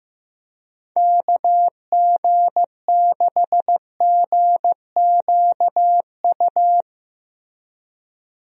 Morse code, heard as a keyed tone, about 15 wpm, 700 Hz. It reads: KG6GQU
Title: USA Call Signs - 6 Characters - Repeated 3x - 15wpm